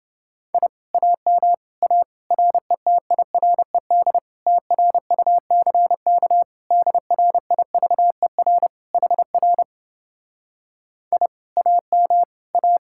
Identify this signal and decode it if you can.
Morse code, heard as a keyed tone, about 30 wpm, 700 Hz. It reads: I AM A RETIRED TRUCK DRIVER HR I AM A